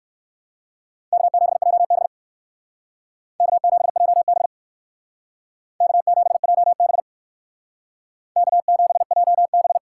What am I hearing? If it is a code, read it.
Morse code, heard as a keyed tone, about 57 wpm, 700 Hz. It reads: K7JB K7JB K7JB K7JB